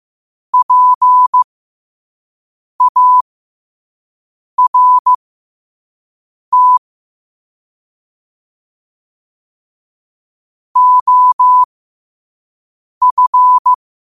Morse code, heard as a keyed tone, about 15 wpm, spaced as for 5 wpm, 1000 Hz. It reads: PART OF
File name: qotdfiles / 5wpm